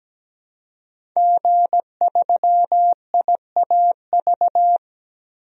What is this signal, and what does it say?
Morse code, heard as a keyed tone, about 17 wpm, 700 Hz. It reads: G3IAV
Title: DXCC-100 Call Signs - 5 Characters - 17wpm